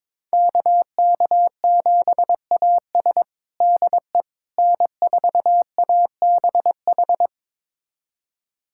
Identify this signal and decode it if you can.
Morse code, heard as a keyed tone, about 22 wpm, 700 Hz. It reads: KK7AS DE N4ABH
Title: USA Callsign DE Callsign - 5 Characters - 22wpm